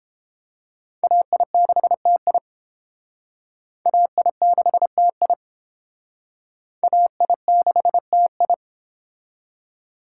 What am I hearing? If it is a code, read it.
Morse code, heard as a keyed tone, about 29 wpm, 700 Hz. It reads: AI6TI AI6TI AI6TI